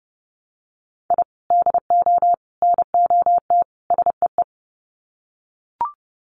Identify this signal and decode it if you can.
Morse code, heard as a keyed tone, about 30 wpm, 700 Hz. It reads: I DO NOT SEE
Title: Sets of 4 Words 30wpm